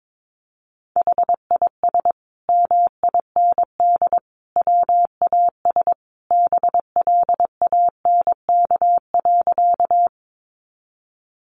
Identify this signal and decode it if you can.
Morse code, heard as a keyed tone, about 22 wpm, 700 Hz. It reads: HIS MIND WAS BLANK.
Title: Sentences from Top 1500 Words 22wpm